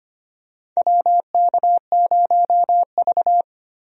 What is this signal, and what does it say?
Morse code, heard as a keyed tone, about 25 wpm, 700 Hz. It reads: WK0V
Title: USA Call Signs - 4 Characters - 25wpm